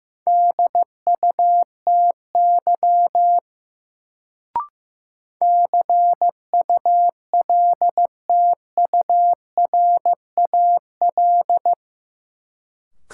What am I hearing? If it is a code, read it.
Morse code, heard as a keyed tone, about 15 wpm, 700 Hz. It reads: DUTY CULTURAL
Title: Sentences from Top 1500 Words - New Word Drill Alt - 15wpm